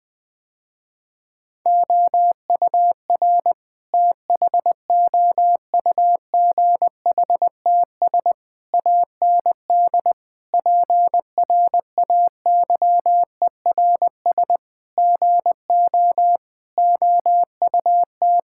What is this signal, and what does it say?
Morse code, heard as a keyed tone, about 20 wpm, 700 Hz. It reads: OUR THOUGHTS AND PRAYERS GO OUT
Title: Sentences with 8-Word Phrases - Spoken Beginning - 20wpm